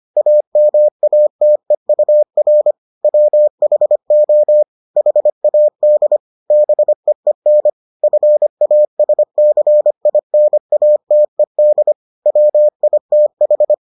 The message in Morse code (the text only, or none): AMATEUR WHO HAD BEEN FASCINATED WITH